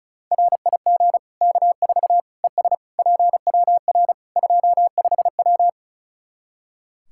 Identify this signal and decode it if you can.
Morse code, heard as a keyed tone, about 36 wpm, 700 Hz. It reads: RIG K4 ES PWR 25W
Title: Practice QSO - Exchange 2 - Info - 35wpm